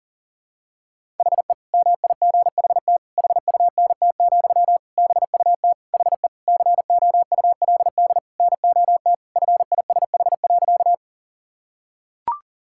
Morse code, heard as a keyed tone, about 40 wpm, 700 Hz. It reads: HE MIGHT HUNT, BUT HE COULD NOT FISH.
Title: Sentences from Top 1600 Words 40wpm